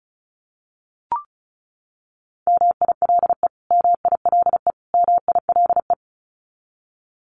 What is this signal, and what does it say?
Morse code, heard as a keyed tone, about 34 wpm, 700 Hz. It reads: MILE MILE MILE
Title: Top 300 Words Repeated 35wpm